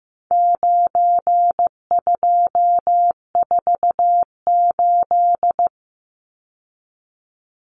Morse code, heard as a keyed tone, about 15 wpm, 700 Hz. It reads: 9248
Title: Numbers - 4-Digit 15wpm